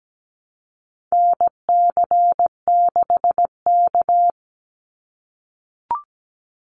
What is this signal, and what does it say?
Morse code, heard as a keyed tone, about 17 wpm, 700 Hz. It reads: NC6K
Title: Call Signs - 17wpm